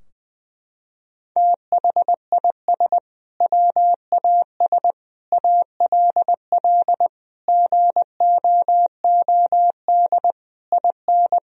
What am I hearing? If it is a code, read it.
Morse code, heard as a keyed tone, about 20 wpm, 700 Hz. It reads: THIS WAS ALL GOOD IN
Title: Sentences from Top 1000 Words - Review Part 3 - 20wpm